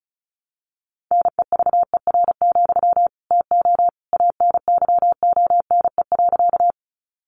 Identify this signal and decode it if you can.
Morse code, heard as a keyed tone, about 34 wpm, 700 Hz. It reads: NEVER, TO ANYONE.
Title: Sentences from Top 900 Words 35wpm